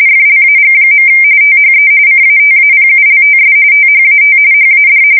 RTTY, about 45.45 baud. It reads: FLAG. THE CATAPULTS THAT HAD
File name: fsk170 / USS-New-Jersey